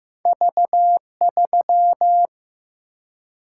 Morse code, heard as a keyed tone, about 15 wpm, 700 Hz. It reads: V3
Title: Single-Letter Single-Number 15wpm